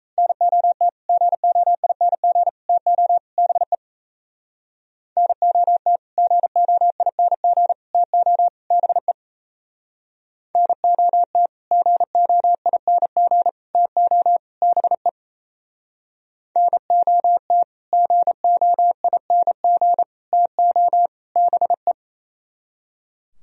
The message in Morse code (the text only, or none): NOT GOING TO BE NOT GOING TO BE NOT GOING TO BE NOT GOING TO BE